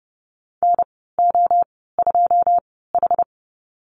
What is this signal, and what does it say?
Morse code, heard as a keyed tone, about 30 wpm, spaced as for 17 wpm, 700 Hz. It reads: NO2H
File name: Lesson 14g - Number 2 - Call Signs 30wpm